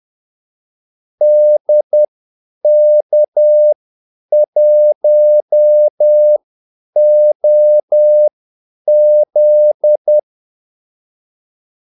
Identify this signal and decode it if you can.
Morse code, heard as a keyed tone, about 10 wpm, spaced as for 8 wpm, 600 Hz. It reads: DK1OZ
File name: Callsigns-10-8wpm